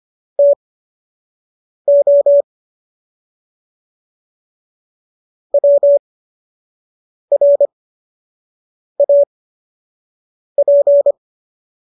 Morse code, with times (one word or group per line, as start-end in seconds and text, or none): 0.4-2.4 TO
5.5-11.1 WRAP